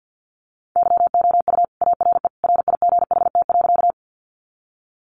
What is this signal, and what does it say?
Morse code, heard as a keyed tone, about 51 wpm, 700 Hz. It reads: YOU ARE RIGHT.
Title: Sentences from Top 200 Words 50wpm